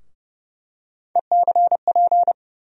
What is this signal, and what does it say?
Morse code, heard as a keyed tone, about 30 wpm, 700 Hz. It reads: ECP
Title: USA Call Sign Postfixes - 3 Letters 30wpm